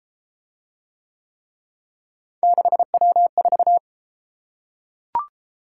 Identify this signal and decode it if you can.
Morse code, heard as a keyed tone, about 31 wpm, 700 Hz. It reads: 6W4